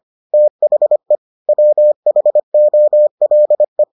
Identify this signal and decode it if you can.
Morse code, heard as a keyed tone, about 25 wpm, 600 Hz. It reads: THE WHOLE